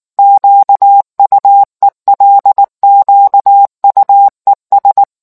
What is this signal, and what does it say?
Morse code, heard as a keyed tone, about 19 wpm, 800 Hz. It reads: QUELQUES